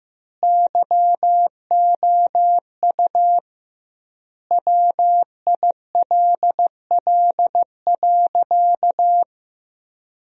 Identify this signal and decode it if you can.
Morse code, heard as a keyed tone, about 15 wpm, 700 Hz. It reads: YOU WILL.
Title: 2x Word-Spacing - Sentences from Top 100 Words - 15wpm